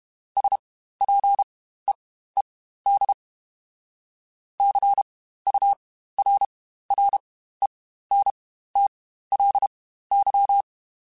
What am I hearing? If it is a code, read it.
Morse code, heard as a keyed tone, about 32 wpm, spaced as for 15 wpm, 800 Hz. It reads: SPEED CURRENTLY
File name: Spaced Characters / Spaced Characters-mp3s / Weather 5 12 sp @30